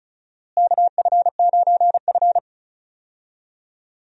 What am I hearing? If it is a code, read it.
Morse code, heard as a keyed tone, about 34 wpm, 700 Hz. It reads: KF9F
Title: USA Call Signs - 4 Characters - Repeated 2x - 35wpm